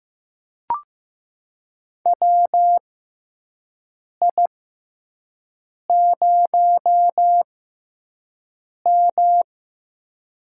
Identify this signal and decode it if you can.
Morse code, heard as a keyed tone, about 15 wpm, spaced as for 5 wpm, 700 Hz. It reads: WI0M
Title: USA Call Signs - 4 Characters - 6x Character Spacing - 15wpm